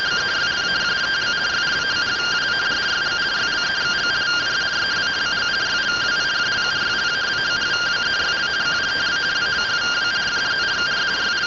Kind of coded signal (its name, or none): two-tone teleprinter signal